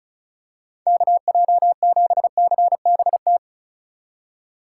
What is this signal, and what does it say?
Morse code, heard as a keyed tone, about 36 wpm, 700 Hz. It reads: KJ7CBT
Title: USA Call Signs - 6x Character Spacing - 35wpm